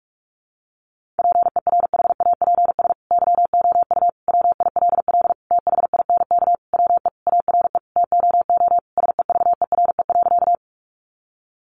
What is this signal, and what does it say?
Morse code, heard as a keyed tone, about 44 wpm, 700 Hz. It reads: PERHAPS YOU WILL THINK WE ARE TOO SEVERE.
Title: Sentences from Top 1400 Words 45wpm